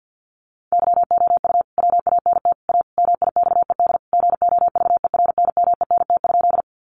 Morse code, heard as a keyed tone, about 50 wpm, 700 Hz. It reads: YOU WANT A MIXED GOVERNMENT?
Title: Sentences from Top 1500 Words 50wpm